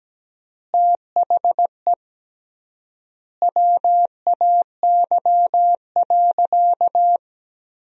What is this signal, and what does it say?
Morse code, heard as a keyed tone, about 17 wpm, 700 Hz. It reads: THE WAY.
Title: 3x Word-Spacing - Sentences from Top 100 Words - 17wpm